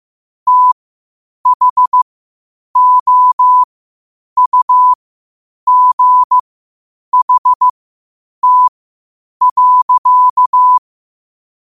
Morse code, heard as a keyed tone, about 15 wpm, spaced as for 8 wpm, 1000 Hz. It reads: THOUGHT.
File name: qotdfiles / 8wpm